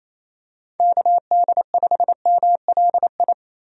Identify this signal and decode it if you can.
Morse code, heard as a keyed tone, about 27 wpm, 700 Hz. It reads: KD5MLI
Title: USA Call Signs - 6 Characters - 28wpm